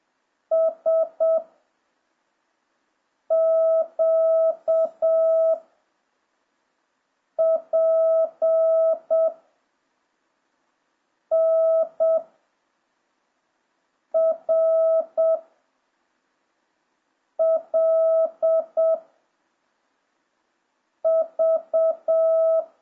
Morse code, heard as a keyed tone, about 7 wpm, 650 Hz. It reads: S Q P N R L V